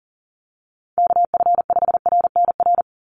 Morse code, heard as a keyed tone, about 40 wpm, 700 Hz. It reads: KF5RNR
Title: USA Call Signs 40wpm